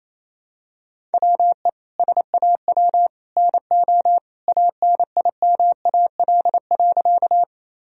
Morse code, heard as a keyed tone, about 28 wpm, 700 Hz. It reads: WE SAW NO ANIMAL.
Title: Sentences from Top 1000 Words - Review Part 1 - 28wpm